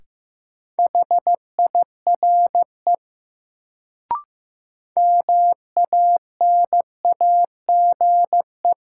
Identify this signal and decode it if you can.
Morse code, heard as a keyed tone, about 15 wpm, 700 Hz. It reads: HIRE MANAGE
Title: Sentences from Top 1400 Words - New Word Drill - 15wpm